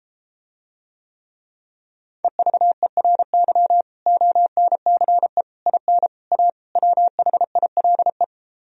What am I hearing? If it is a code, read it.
Morse code, heard as a keyed tone, about 33 wpm, 700 Hz. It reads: EVERY ONCE IN A WHILE